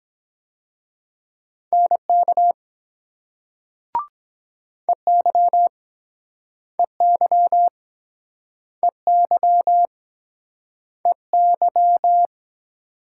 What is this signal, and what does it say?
Morse code, heard as a keyed tone, about 25 wpm, 700 Hz. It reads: NK EY EY EY EY